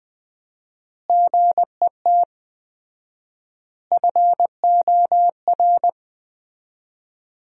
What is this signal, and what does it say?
Morse code, heard as a keyed tone, about 20 wpm, 700 Hz. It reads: GET FOR
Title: 4x Word-Spacing - Sentences from Top 400 Words - 20wpm